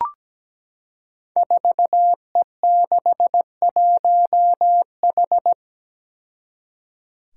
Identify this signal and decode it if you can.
Morse code, heard as a keyed tone, about 17 wpm, 700 Hz. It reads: 4E61H
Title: Five Challenging Letter-Number Combinations 17wpm